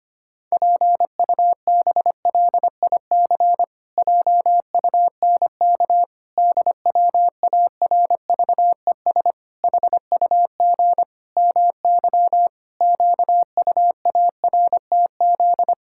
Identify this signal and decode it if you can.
Morse code, heard as a keyed tone, about 25 wpm, 700 Hz. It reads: PUBLIC JUNK DWARVES HUG MY QUARTZ
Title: Pangram Challenge v2 - 25wpm